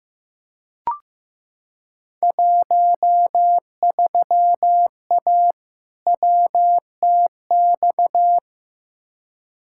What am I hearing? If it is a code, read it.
Morse code, heard as a keyed tone, about 15 wpm, 700 Hz. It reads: 13A WTX
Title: Field Day Exchanges Encore - 15wpm